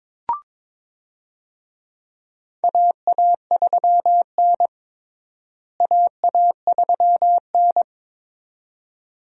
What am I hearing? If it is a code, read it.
Morse code, heard as a keyed tone, about 22 wpm, 700 Hz. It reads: AA3N AA3N
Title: USA Call Signs - 4 Characters - Repeated 2x - 22wpm